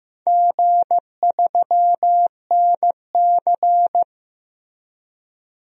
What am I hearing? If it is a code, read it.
Morse code, heard as a keyed tone, about 15 wpm, 700 Hz. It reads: G3NC
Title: DXCC-100 Call Signs - 4 Characters - Repeated 3x - 15wpm